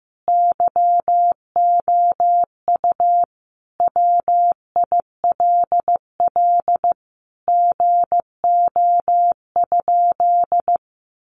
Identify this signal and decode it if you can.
Morse code, heard as a keyed tone, about 15 wpm, 700 Hz. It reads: YOU WILL GO?
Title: Sentences from Top 1300 Words 15wpm